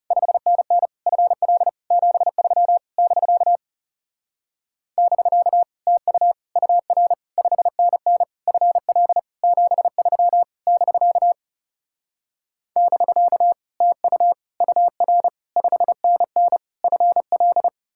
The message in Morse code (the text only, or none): 5NN FL 73 <BK> <BK> TU UR 5NN FL 73 <BK> <BK> TU UR 5NN FL